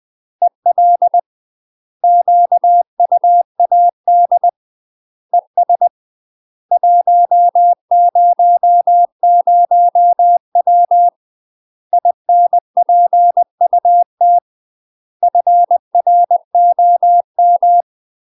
Morse code, heard as a keyed tone, about 20 wpm, 700 Hz. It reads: EL QUAD ES 100W INPUT FROM
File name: UK MASTER QSOs TRAINING / 20WPM / UK MASTER QSOs 0031